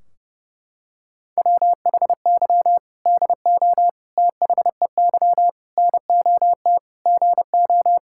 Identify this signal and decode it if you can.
Morse code, heard as a keyed tone, about 30 wpm, 700 Hz. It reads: WHY DO THEY NOT GO
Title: Sentences from Top 1600 Words 30wpm